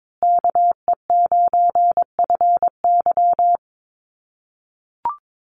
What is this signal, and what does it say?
Morse code, heard as a keyed tone, about 22 wpm, 700 Hz. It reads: KE9FY
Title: USA Call Signs Repeated 2x 22wpm